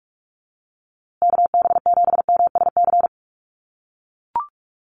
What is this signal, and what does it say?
Morse code, heard as a keyed tone, about 44 wpm, 700 Hz. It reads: KB7MSC
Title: USA Call Signs Repeated 2x 45wpm